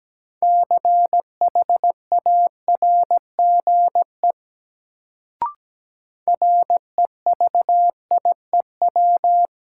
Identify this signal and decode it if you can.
Morse code, heard as a keyed tone, about 17 wpm, 700 Hz. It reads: CHARGE REVIEW
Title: Sentences from Top 900 Words - New Word Drill - 17wpm